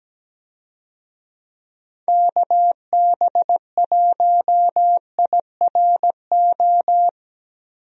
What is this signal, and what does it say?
Morse code, heard as a keyed tone, about 17 wpm, 700 Hz. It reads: KB1IRO
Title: USA Call Signs - 6 Characters - Repeated 3x - 17wpm